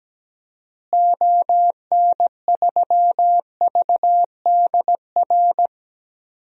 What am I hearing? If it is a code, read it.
Morse code, heard as a keyed tone, about 17 wpm, 700 Hz. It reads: ON3VDR
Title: DXCC-100 Call Signs - 6 Characters - Repeated 2x - 17wpm